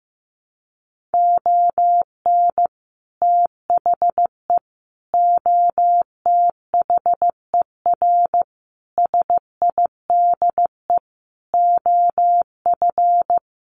Morse code, heard as a keyed tone, about 15 wpm, 700 Hz. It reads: ON THE OTHER SIDE OF
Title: Sets of 5 Words 15wpm